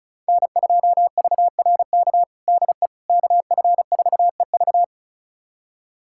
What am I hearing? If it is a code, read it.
Morse code, heard as a keyed tone, about 36 wpm, 700 Hz. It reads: N2VRK DE KF4EV
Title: USA Callsign DE Callsign - 5 Characters - 35wpm